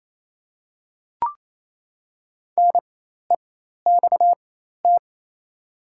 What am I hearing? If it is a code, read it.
Morse code, heard as a keyed tone, about 28 wpm, spaced as for 13 wpm, 700 Hz. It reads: NEXT
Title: 4x Character Spacing - Top 500 Words - 28wpm